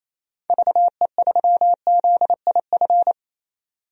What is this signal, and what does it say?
Morse code, heard as a keyed tone, about 28 wpm, 700 Hz. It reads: VE3ZIF
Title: DXCC-100 Call Signs - 6 Characters - 28wpm